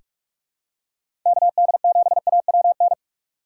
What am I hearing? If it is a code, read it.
Morse code, heard as a keyed tone, about 46 wpm, 700 Hz. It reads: KD7AWN
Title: USA Call Signs - 6 Characters - 3x Character Spacing - 45wpm